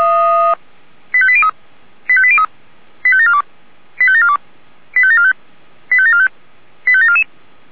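Five-tone selective call (ZVEI), heard as quick runs of five tones, tones of about 70 ms. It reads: 85803 85803 85742 85742 85745 85745 85744
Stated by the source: ZVEI five-tone call received on an SDR.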